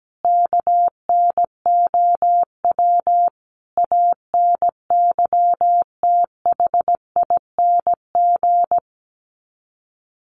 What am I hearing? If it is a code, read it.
Morse code, heard as a keyed tone, about 17 wpm, 700 Hz. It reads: KNOW ANYTHING
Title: Sets of 5 Words Encore 17wpm